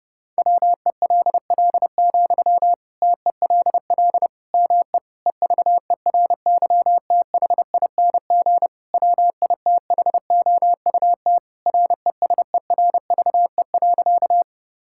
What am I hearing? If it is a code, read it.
Morse code, heard as a keyed tone, about 30 wpm, 700 Hz. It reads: WELL, TELL ME EVERYTHING WITHOUT RESERVE.
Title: Sentences from Top 1000 Words - Review Part 6 - 30wpm